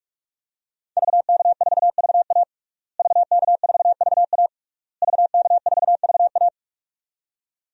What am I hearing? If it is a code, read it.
Morse code, heard as a keyed tone, about 46 wpm, 700 Hz. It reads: VK4VA VK4VA VK4VA
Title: DXCC-100 Call Signs - 5 Characters - Repeated 3x - 45wpm